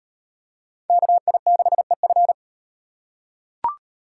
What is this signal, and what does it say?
Morse code, heard as a keyed tone, about 35 wpm, 700 Hz. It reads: KI6EF